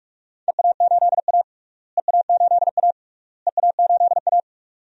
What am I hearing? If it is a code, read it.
Morse code, heard as a keyed tone, about 46 wpm, 700 Hz. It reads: EA8A EA8A EA8A
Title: DXCC-100 Call Signs - 4 Characters - Repeated 3x - 45wpm